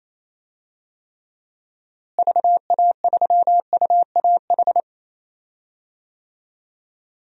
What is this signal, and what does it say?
Morse code, heard as a keyed tone, about 28 wpm, 700 Hz. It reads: VA3UAH
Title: DXCC-100 Call Signs - 6 Characters - Repeated 3x - 28wpm